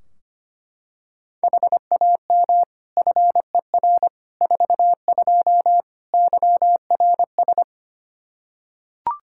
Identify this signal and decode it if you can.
Morse code, heard as a keyed tone, about 25 wpm, 700 Hz. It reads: HAM FER 42 YRS
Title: Practice QSO - Exchange 3 - More Info - 25wpm